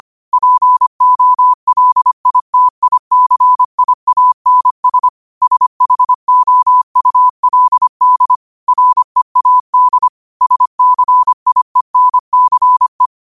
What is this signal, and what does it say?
Morse code, heard as a keyed tone, about 25 wpm, 1000 Hz. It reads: POLITICIANS SHOULD READ SCIENCE
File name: qotdfiles / 25wpm